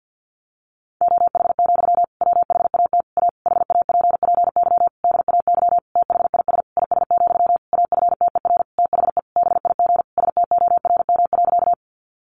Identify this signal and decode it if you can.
Morse code, heard as a keyed tone, about 50 wpm, 700 Hz. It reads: OH, WHAT A HAPPY DAY THIS IS, AFTER THE BIG STORM.
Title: Sentences from Top 1500 Words 50wpm